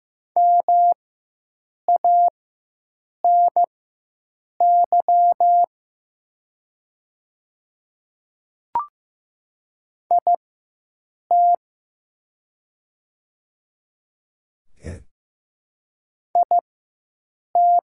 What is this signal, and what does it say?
Morse code, heard as a keyed tone, about 15 wpm, spaced as for 7 wpm, 700 Hz. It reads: MANY IT IT
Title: 4x Character Spacing - Top 300 Words - 15wpm